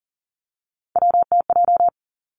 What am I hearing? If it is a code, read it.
Morse code, heard as a keyed tone, about 39 wpm, 700 Hz. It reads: WTJ
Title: Three Challenging Letter-Number Combinations 40wpm